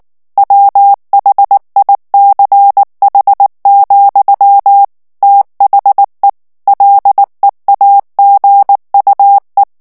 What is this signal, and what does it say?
Morse code, heard as a keyed tone, about 19 wpm, 800 Hz. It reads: WHICH, THE LEAGUE
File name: ANGLAIS texte clair 3 vitesse 18 mots minute